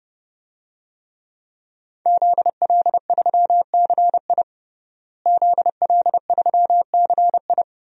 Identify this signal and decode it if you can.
Morse code, heard as a keyed tone, about 30 wpm, 700 Hz. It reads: ZL3CI ZL3CI
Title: DXCC-100 Call Signs - 5 Characters - Repeated 2x - 30wpm